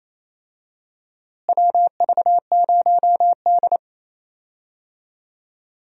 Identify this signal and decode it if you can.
Morse code, heard as a keyed tone, about 28 wpm, 700 Hz. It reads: WV0D
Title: USA Call Signs - 4 Characters - Repeated 3x - 28wpm